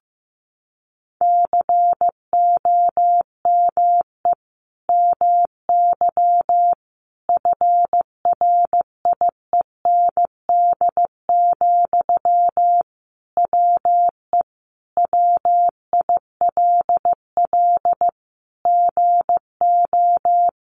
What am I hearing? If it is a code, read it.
Morse code, heard as a keyed tone, about 14 wpm, 700 Hz. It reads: COME MY FRIEND, WE WILL GO
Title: Sentences from Top 1300 Words 15wpm